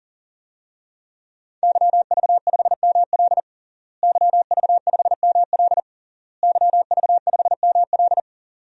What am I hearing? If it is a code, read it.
Morse code, heard as a keyed tone, about 40 wpm, 700 Hz. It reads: YV5ML YV5ML YV5ML
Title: DXCC-100 Call Signs - 5 Characters - Repeated 3x - 40wpm